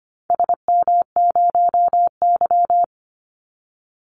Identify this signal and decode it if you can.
Morse code, heard as a keyed tone, about 25 wpm, 700 Hz. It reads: SM0Y
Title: Call Signs - 25wpm